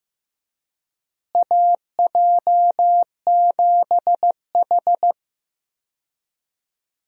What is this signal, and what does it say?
Morse code, heard as a keyed tone, about 15 wpm, 700 Hz. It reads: AJ7H
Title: USA Call Signs - 4 Characters - Repeated 2x - 15wpm